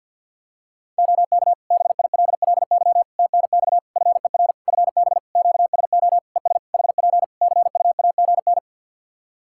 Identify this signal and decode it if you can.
Morse code, heard as a keyed tone, about 50 wpm, 700 Hz. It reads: OK BILLY TNX FER FB QSO ES HP CUAGN
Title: Practice QSO - Exchange 4 - Ending - 50wpm